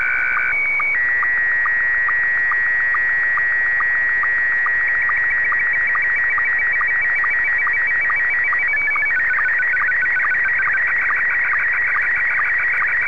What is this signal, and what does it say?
SSTV (slow-scan television), Scottie 1.